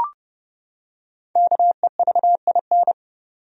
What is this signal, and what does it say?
Morse code, heard as a keyed tone, about 30 wpm, 700 Hz. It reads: KEVIN
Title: USA Common Names - Max 5 Letters - 30wpm